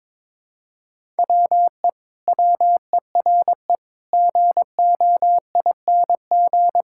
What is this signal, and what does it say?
Morse code, heard as a keyed tone, about 22 wpm, 700 Hz. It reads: WE WERE GOING
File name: practice / 22wpm / Sentences with 8-Word Phrases - 22wpm